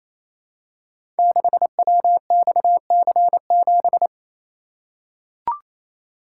Morse code, heard as a keyed tone, about 29 wpm, 700 Hz. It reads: 6WXC7
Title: Code Groups - 5 Letter:Numbers - 28wpm